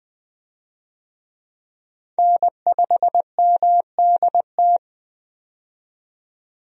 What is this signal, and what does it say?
Morse code, heard as a keyed tone, about 21 wpm, 700 Hz. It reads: N5MDT
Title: USA Call Signs - 5 Characters - 20wpm